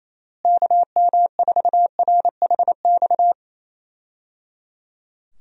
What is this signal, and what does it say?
Morse code, heard as a keyed tone, about 28 wpm, 700 Hz. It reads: KM4RHX
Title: USA Call Signs - 6 Characters - Repeated 3x - 28wpm